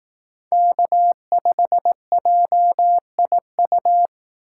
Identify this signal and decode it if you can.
Morse code, heard as a keyed tone, about 18 wpm, 700 Hz. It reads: K5JIU
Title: Speed Racing - USA 1x3 Call Signs 15wpm